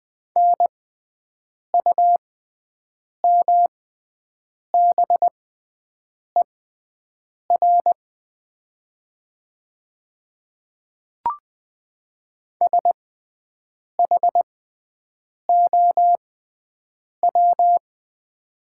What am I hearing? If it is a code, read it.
Morse code, heard as a keyed tone, about 20 wpm, spaced as for 7 wpm, 700 Hz. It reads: NUMBER SHOW
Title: 6x Character Spacing - Top 200 Words - 20wpm